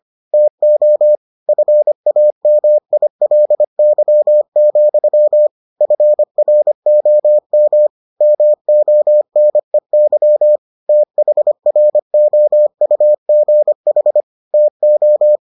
Morse code, heard as a keyed tone, about 25 wpm, 600 Hz. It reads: TO FAMILY, FROM MONEY THROUGH TO